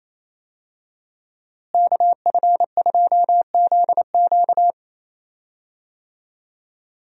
Morse code, heard as a keyed tone, about 28 wpm, 700 Hz. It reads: KF2ZQ
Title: USA Call Signs - 5 Characters - Repeated 3x - 28wpm